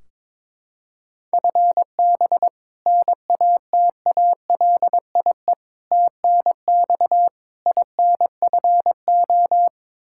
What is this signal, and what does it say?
Morse code, heard as a keyed tone, about 22 wpm, 700 Hz. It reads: FB NATALIE TNX INFO
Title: Practice QSO - Exchange 3 - More Info - 22wpm